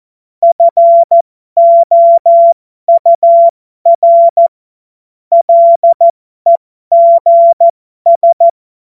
Morse code, heard as a keyed tone, about 14 wpm, spaced as for 12 wpm, 680 Hz. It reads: FOUR LEGS
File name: Morsekurs / Witze